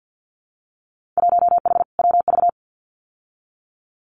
Morse code, heard as a keyed tone, about 50 wpm, 700 Hz. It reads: 1H WV
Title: Winter Field Day Exchanges 50wpm